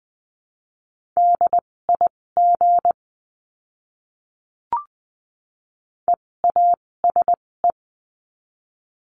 Morse code, heard as a keyed tone, about 20 wpm, spaced as for 16 wpm, 700 Hz. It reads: DIG EASE
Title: 256 Lesson 42a - Words 12-20wpm